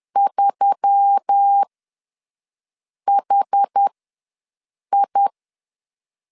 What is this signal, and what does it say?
Morse code, heard as a keyed tone, about 10 wpm, spaced as for 5 wpm, 800 Hz. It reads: 3HI
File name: LEZIONI / 5 LBVC / 5 LBVC ws 10wpm 3733